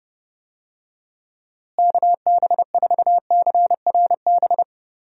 Morse code, heard as a keyed tone, about 30 wpm, 700 Hz. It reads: KB4CRB
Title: USA Call Signs - 6 Characters - Repeated 2x - 30wpm